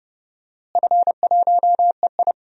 Morse code, heard as a keyed tone, about 32 wpm, 700 Hz. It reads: F1EI
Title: DXCC-100 Call Signs - 4 Characters - Repeated 2x - 30wpm